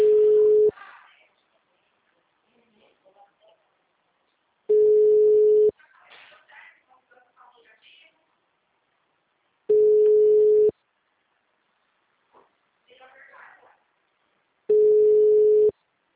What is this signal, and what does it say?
Ringback tone (Europe, 425 Hz), with speech.
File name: out- -2003- - - .8424